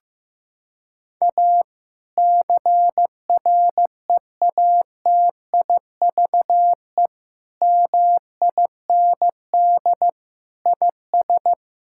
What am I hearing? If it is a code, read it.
Morse code, heard as a keyed tone, about 16 wpm, 700 Hz. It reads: A CREATIVE MIND IS
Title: Sentences from Top 1000 Words - Review Part 5 - 15wpm